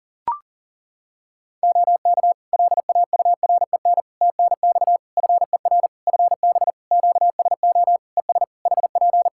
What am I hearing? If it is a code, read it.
Morse code, heard as a keyed tone, about 39 wpm, 700 Hz. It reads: OK LAUREN TNX FER FB QSO ES HP
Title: Practice QSO - Exchange 4 - Ending - 40wpm